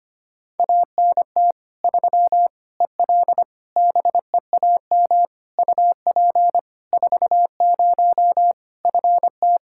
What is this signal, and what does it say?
Morse code, heard as a keyed tone, about 25 wpm, 700 Hz. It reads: ANT 3 EL BEAM UP 40 FT
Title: K1USN SST QSO Practice 25wpm